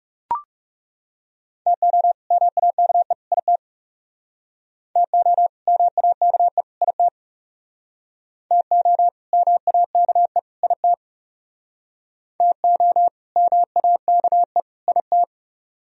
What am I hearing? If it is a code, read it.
Morse code, heard as a keyed tone, about 39 wpm, 700 Hz. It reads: TO MAKE IT TO MAKE IT TO MAKE IT TO MAKE IT